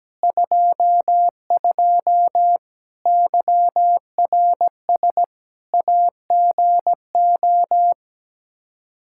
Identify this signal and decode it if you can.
Morse code, heard as a keyed tone, about 17 wpm, 700 Hz. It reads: 22 YRS AGO
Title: Practice QSO - Exchange 3 - More Info - 17wpm